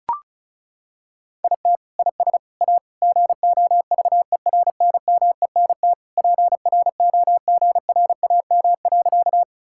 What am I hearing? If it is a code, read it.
Morse code, heard as a keyed tone, about 36 wpm, 700 Hz. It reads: IT IS A GOVERNMENT PROGRAM.
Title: Sentences from Top 1000 Words - Review Part 5 - 35wpm